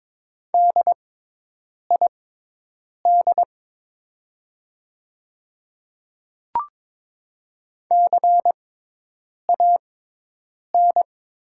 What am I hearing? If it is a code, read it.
Morse code, heard as a keyed tone, about 22 wpm, spaced as for 8 wpm, 700 Hz. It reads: DID CAN
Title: 6x Character Spacing - Top 100 Words - 22wpm